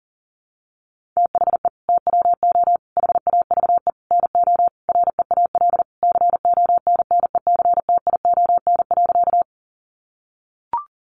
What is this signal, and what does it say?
Morse code, heard as a keyed tone, about 40 wpm, 700 Hz. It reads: THE TWO HAVE NO REAL CONNECTION.
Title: Sentences from Top 1300 Words 40wpm